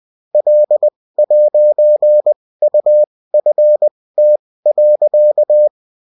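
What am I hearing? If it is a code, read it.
Morse code, heard as a keyed tone, about 20 wpm, spaced as for 16 wpm, 600 Hz. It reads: L'UFT.
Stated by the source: UFT bulletin sent in CW.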